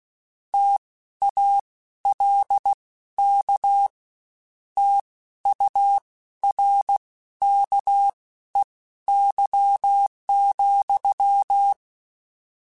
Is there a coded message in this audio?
Yes: Morse code from 0.5 to 11.7 s, heard as a keyed tone, about 16 wpm, spaced as for 12 wpm, 800 Hz. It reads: TALK TURKEY,